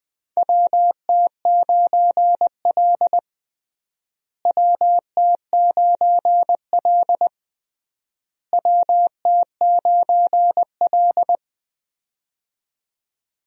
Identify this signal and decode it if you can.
Morse code, heard as a keyed tone, about 21 wpm, 700 Hz. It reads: WT9L WT9L WT9L
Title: USA Call Signs - 4 Characters - Repeated 3x - 20wpm